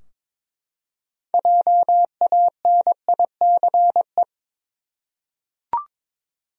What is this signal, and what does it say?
Morse code, heard as a keyed tone, about 22 wpm, 700 Hz. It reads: JANICE
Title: USA Common Names - 22wpm